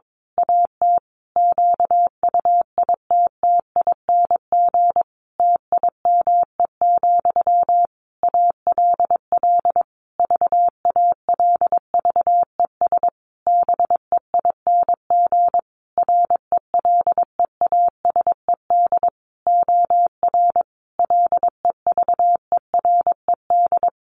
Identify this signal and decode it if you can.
Morse code, heard as a keyed tone, about 22 wpm, 700 Hz. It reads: AT QUITTING TIME, ALL VALVES BEING RELEASED OR LEVERED
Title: A Sound Of Thunder 22wpm